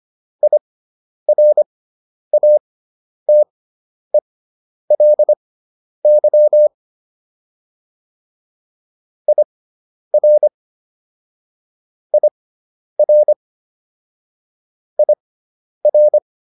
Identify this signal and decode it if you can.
Morse code, heard as a keyed tone, about 25 wpm, spaced as for 10 wpm, 600 Hz. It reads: IRATELY IR IR IR